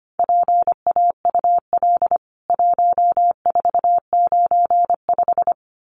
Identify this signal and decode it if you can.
Morse code, heard as a keyed tone, about 25 wpm, 700 Hz. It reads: PAUL 1495
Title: CWT Exchange 25wpm